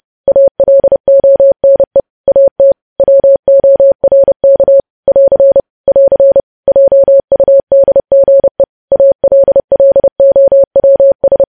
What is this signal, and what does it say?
Morse code, heard as a keyed tone, about 30 wpm, 550 Hz. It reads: ALONE AT WORK <AR> <AR> JUDGE ALLOWS
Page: news headlines in morse code at 30 wpm